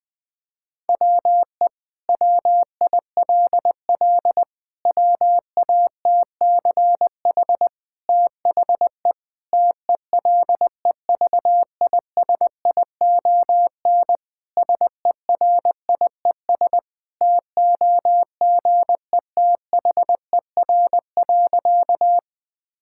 Morse code, heard as a keyed tone, about 20 wpm, 700 Hz. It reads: WE WILL WATCH THE TELEVISION SERIES TOGETHER.
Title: Sentences from Top 1000 Words - Review Part 5 - 20wpm